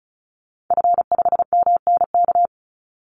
Morse code, heard as a keyed tone, about 34 wpm, 700 Hz. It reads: F5MNK
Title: Call Signs - 35wpm